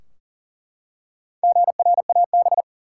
Morse code, heard as a keyed tone, about 40 wpm, 700 Hz. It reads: GRAB
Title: Four Letter Words - 40wpm